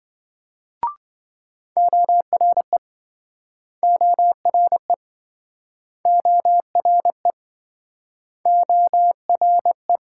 Morse code, heard as a keyed tone, about 26 wpm, 700 Hz. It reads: ORE ORE ORE ORE